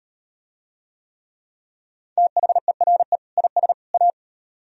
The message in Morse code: THERE IS A

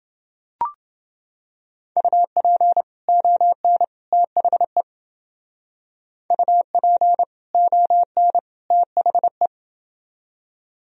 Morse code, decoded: UP ON THE UP ON THE

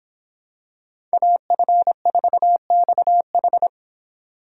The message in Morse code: AF4XH

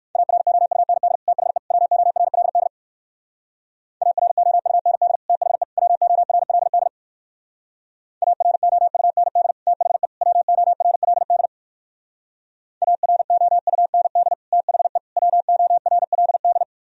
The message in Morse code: AROUND THE WORLD AROUND THE WORLD AROUND THE WORLD AROUND THE WORLD

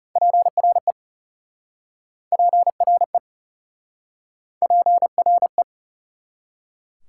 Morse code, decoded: PRE PRE PRE

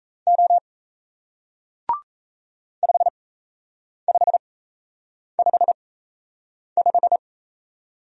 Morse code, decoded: O 5555

